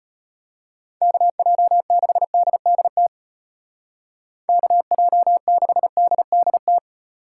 Morse code, heard as a keyed tone, about 33 wpm, 700 Hz. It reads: KJ6DDT KJ6DDT